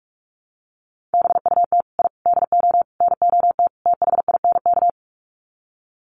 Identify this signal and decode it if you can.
Morse code, heard as a keyed tone, about 44 wpm, 700 Hz. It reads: BUT I DO NOT THINK
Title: Sets of 5 Words 45wpm